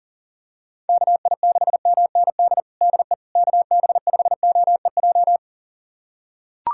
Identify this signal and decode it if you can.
Morse code, heard as a keyed tone, about 37 wpm, 700 Hz. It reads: KI6MND DE KB5OEJ
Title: USA Callsign DE Callsign 40wpm